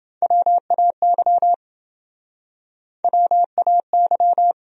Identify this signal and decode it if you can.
Morse code, heard as a keyed tone, about 28 wpm, 700 Hz. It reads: WAY WAY